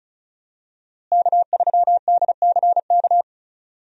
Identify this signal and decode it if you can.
Morse code, heard as a keyed tone, about 36 wpm, 700 Hz. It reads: K3DCK
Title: USA Call Signs - 5 Characters - Repeated 2x - 35wpm